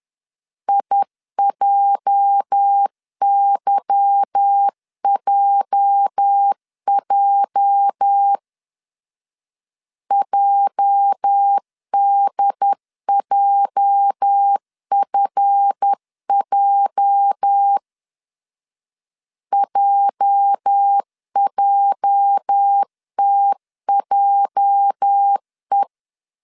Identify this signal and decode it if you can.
Morse code, heard as a keyed tone, about 10 wpm, 800 Hz. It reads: IJYJJ JDJFJ JJTJE
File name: LEZIONI / 8 YJ45 / 8 YJ 10wpm 2651